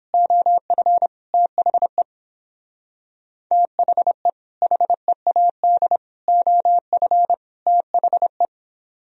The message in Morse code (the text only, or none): OF THE THE HEAD OF THE